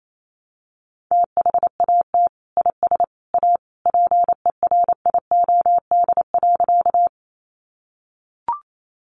Morse code, heard as a keyed tone, about 28 wpm, 700 Hz. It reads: THAT IS A PERIOD.
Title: Sentences from Top 800 Words 28wpm